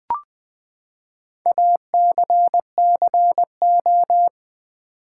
Morse code, heard as a keyed tone, about 20 wpm, 700 Hz. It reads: ACCO
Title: Build A Word - 4 Letters At A Time - 10-Letter Words - 20wpm